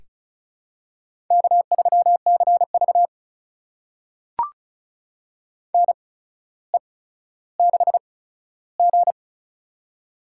Morse code, heard as a keyed tone, about 36 wpm, spaced as for 15 wpm, 700 Hz. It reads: K3CV NE6G
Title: USA Call Signs - 4 Characters - 8x Character Spacing - 35wpm